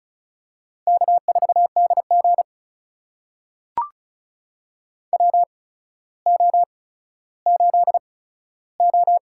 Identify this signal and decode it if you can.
Morse code, heard as a keyed tone, about 36 wpm, spaced as for 15 wpm, 700 Hz. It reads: K4DG WO8O
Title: USA Call Signs - 4 Characters - 8x Character Spacing - 35wpm